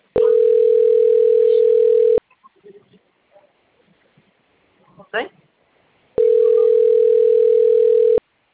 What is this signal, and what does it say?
Ringback tone (North America), 440 + 480 Hz, with speech.
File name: out- - - - - .98734